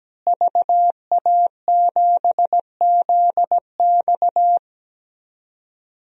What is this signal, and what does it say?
Morse code, heard as a keyed tone, about 17 wpm, 700 Hz. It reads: VA7ZX
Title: DXCC-100 Call Signs - 5 Characters - Repeated 2x - 17wpm